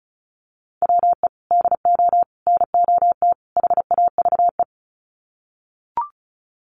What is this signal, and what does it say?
Morse code, heard as a keyed tone, about 34 wpm, 700 Hz. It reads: WE DO NOT HAVE
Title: Sets of 4 Words 35wpm